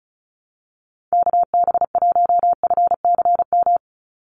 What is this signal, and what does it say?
Morse code, heard as a keyed tone, about 36 wpm, 700 Hz. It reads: KB1FCM
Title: USA Call Signs 35wpm